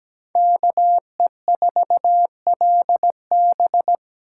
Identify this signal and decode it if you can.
Morse code, heard as a keyed tone, about 20 wpm, 700 Hz. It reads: KE4LB
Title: Speed Racing - USA Call Signs 17wpm